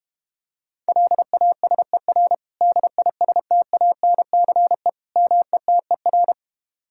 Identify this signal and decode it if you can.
Morse code, heard as a keyed tone, about 33 wpm, 700 Hz. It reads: LASER DISTANCE METER